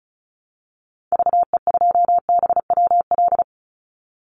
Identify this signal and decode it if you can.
Morse code, heard as a keyed tone, about 34 wpm, 700 Hz. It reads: VE2BWL
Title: Call Signs Repeated 3x - 35wpm